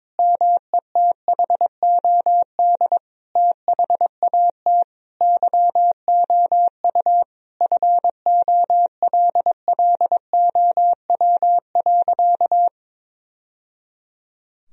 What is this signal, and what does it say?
Morse code, heard as a keyed tone, about 22 wpm, 700 Hz. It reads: METHOD THAT YOU FOLLOW.
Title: Sentences from Top 1000 Words - Review Part 3 - 22wpm